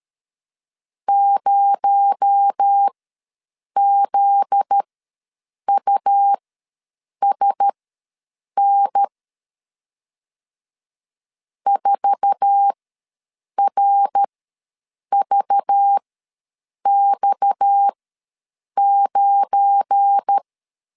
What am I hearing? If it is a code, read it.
Morse code, heard as a keyed tone, about 13 wpm, spaced as for 7 wpm, 800 Hz. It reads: 0ZUSN 4RVX9
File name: LEZIONI / 8 YJ45 / 8 YJ4 ws 12wpm 3249